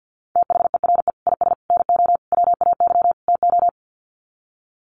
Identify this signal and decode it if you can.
Morse code, heard as a keyed tone, about 50 wpm, 700 Hz. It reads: THERE IS NO WAY TO
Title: Sets of 5 Words Encore 50wpm